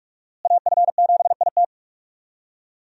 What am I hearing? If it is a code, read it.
Morse code, heard as a keyed tone, about 46 wpm, 700 Hz. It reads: AF7IT
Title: USA Call Signs - 5 Characters - Repeated 3x - 45wpm